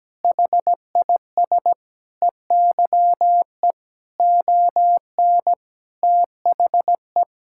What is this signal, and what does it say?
Morse code, heard as a keyed tone, about 17 wpm, 700 Hz. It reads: HIS EYE ON THE
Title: Sentences from Top 1000 Words - Review Part 5 - 17wpm